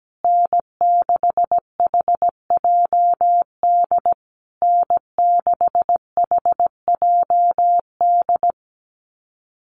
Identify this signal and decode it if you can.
Morse code, heard as a keyed tone, about 17 wpm, 700 Hz. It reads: N6HJD N6HJD
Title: USA Call Signs Repeated 3x 17wpm